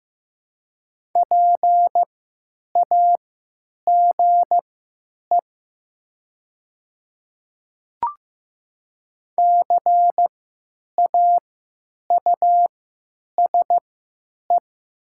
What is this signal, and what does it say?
Morse code, heard as a keyed tone, about 15 wpm, spaced as for 9 wpm, 700 Hz. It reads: PAGE CAUSE
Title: 3x Character Spacing - Top 500 Words - 15wpm